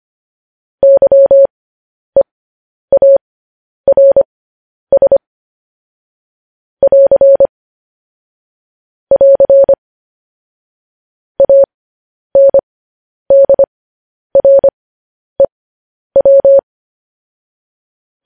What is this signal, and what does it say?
Morse code, heard as a keyed tone, about 25 wpm, spaced as for 10 wpm, 550 Hz. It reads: YEARS <AR> <AR> ANDREW
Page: news headlines in morse code at 10 wpm